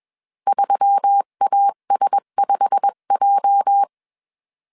Morse code, heard as a keyed tone, about 21 wpm, 800 Hz. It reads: 3AS5J